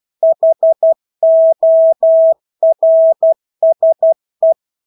Morse code, heard as a keyed tone, about 12 wpm, 650 Hz. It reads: HORSE
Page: Engelske ord 12 wpm